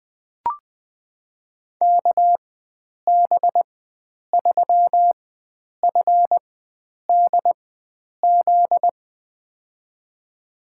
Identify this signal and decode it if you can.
Morse code, heard as a keyed tone, about 21 wpm, spaced as for 9 wpm, 700 Hz. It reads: KB3FDZ
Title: USA Call Signs - 6 Characters - 4x Character Spacing - 20wpm